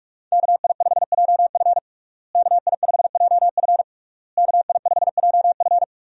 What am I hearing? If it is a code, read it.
Morse code, heard as a keyed tone, about 46 wpm, 700 Hz. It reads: KI5JF KI5JF KI5JF
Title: USA Call Signs - 5 Characters - Repeated 3x - 45wpm